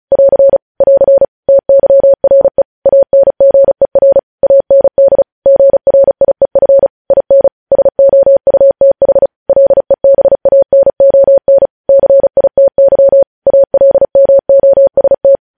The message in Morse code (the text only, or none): <AR> <AR> TYRE ANGER AND GRIEF IN SOUTH LEBANON CITY ALMOST